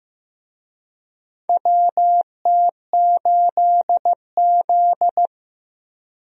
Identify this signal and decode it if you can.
Morse code, heard as a keyed tone, about 15 wpm, 700 Hz. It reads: WT8Z
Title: USA Call Signs - 4 Characters - Repeated 3x - 15wpm